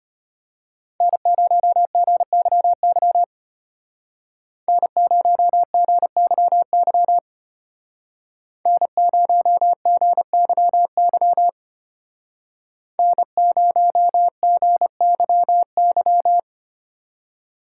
Morse code, heard as a keyed tone, about 30 wpm, 700 Hz. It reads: N0GYY N0GYY N0GYY N0GYY
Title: Speed Racing - USA 1x3 Call Signs 25wpm